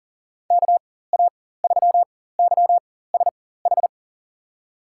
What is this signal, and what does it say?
Morse code, heard as a keyed tone, about 37 wpm, spaced as for 18 wpm, 700 Hz. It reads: KA3YSH
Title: USA Call Signs - 6 Characters - 4x Character Spacing - 40wpm